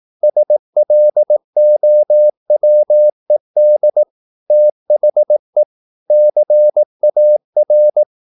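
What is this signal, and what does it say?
Morse code, heard as a keyed tone, about 18 wpm, 600 Hz. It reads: SLOWED THE CAR
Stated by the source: Morse practice at 18 wpm, English text, from CW med Gnister.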